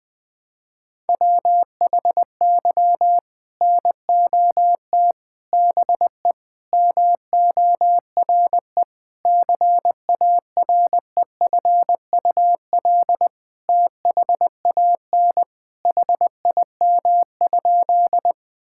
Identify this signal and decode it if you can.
Morse code, heard as a keyed tone, about 21 wpm, 700 Hz. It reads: WHY NOT BE MORE CAREFUL THAN HIM?
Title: Sentences from Top 1600 Words 20wpm